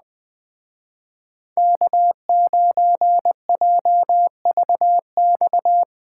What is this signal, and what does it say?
Morse code, heard as a keyed tone, about 20 wpm, 700 Hz. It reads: K9JVX
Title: Speed Racing - USA 1x3 Call Signs 17wpm